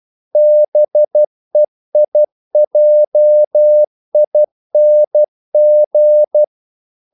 Morse code, heard as a keyed tone, about 12 wpm, 600 Hz. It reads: BEIJING
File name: CW / Practice / NationalCapitals12